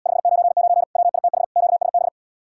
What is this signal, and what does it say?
Morse code, heard as a keyed tone, about 79 wpm, 700 Hz. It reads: 599 WITH QSB